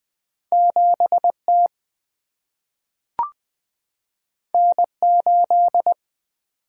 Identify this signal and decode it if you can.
Morse code, heard as a keyed tone, about 20 wpm, 700 Hz. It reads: 7T N8
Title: Single-Letter Single-Number 20wpm